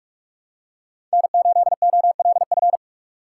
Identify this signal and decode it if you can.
Morse code, heard as a keyed tone, about 46 wpm, 700 Hz. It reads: N8OLF
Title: USA Call Signs - 5 Characters - 2x Character Spacing - 45wpm